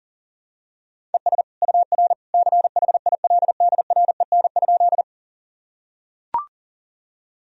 Morse code, heard as a keyed tone, about 40 wpm, 700 Hz. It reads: ES UR CHILDREN?
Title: Sentences from Top 300 Words and Abbreviations 40wpm